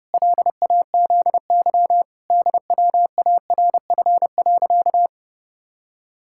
Morse code, heard as a keyed tone, about 30 wpm, 700 Hz. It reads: LAZY DWARF.
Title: Pangram Challenge v2 - 30wpm